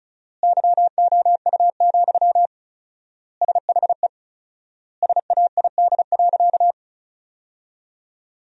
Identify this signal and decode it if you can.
Morse code, heard as a keyed tone, about 36 wpm, 700 Hz. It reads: YOU, SHE SAID.
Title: 4x Word-Spacing - Sentences from Top 400 Words - 35wpm